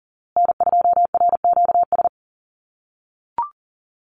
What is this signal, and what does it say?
Morse code, heard as a keyed tone, about 37 wpm, 700 Hz. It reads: N2RQS
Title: USA Call Signs Repeated 2x 40wpm